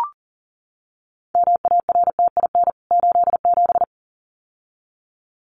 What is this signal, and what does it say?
Morse code, heard as a keyed tone, about 40 wpm, 700 Hz. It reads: MARTIN 87
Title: CWT Exchange 40wpm